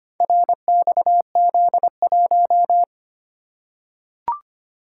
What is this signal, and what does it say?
Morse code, heard as a keyed tone, about 25 wpm, 690 Hz. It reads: RXZ1
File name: practice / 25wpm / Four Letter-Number Combinations - 25wpm